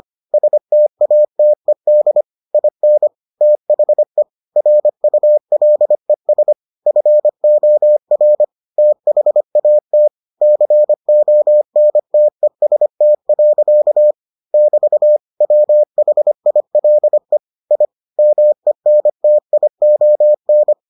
Morse code, heard as a keyed tone, about 25 wpm, 600 Hz. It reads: STATED IN THE RULES FOR THAT CONTEST. <BT> WHILE I MENTION